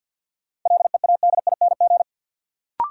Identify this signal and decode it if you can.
Morse code, heard as a keyed tone, about 51 wpm, 700 Hz. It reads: LEADING
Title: Sentences from Top 1100 Words - New Word Drill - 50wpm